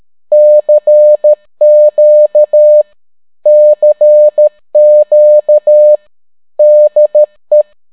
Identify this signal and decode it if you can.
Morse code, heard as a keyed tone, about 13 wpm, 600 Hz. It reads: CQ CQ DE